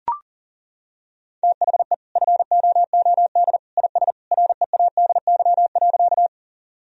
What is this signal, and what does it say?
Morse code, heard as a keyed tone, about 40 wpm, 700 Hz. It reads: THE FOOD IS READY.
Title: Sentences from Top 1000 Words - Review Part 1 - 40wpm